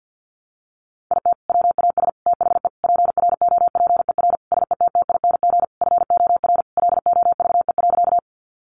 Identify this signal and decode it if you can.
Morse code, heard as a keyed tone, about 50 wpm, 700 Hz. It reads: IT WAS THE PROPER SETTING FOR LOVE.
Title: Sentences from Top 1300 Words 50wpm